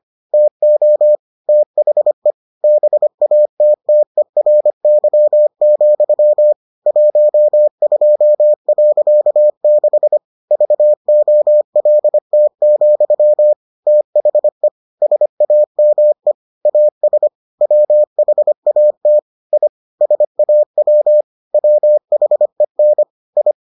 Morse code, heard as a keyed tone, about 25 wpm, 600 Hz. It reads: TO THE BATTERY, 12.6 VOLT, THE SAME AS WHAT I SAW WHEN I